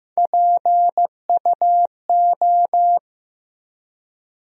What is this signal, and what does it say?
Morse code, heard as a keyed tone, about 15 wpm, 700 Hz. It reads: PUO
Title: USA Call Sign Postfixes - 3 Letters 15wpm